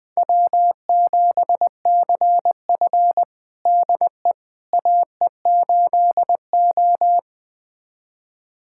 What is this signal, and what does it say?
Morse code, heard as a keyed tone, about 20 wpm, 700 Hz. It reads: W7CF DE AE8O
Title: USA Callsign DE Callsign - 4 Characters - 20wpm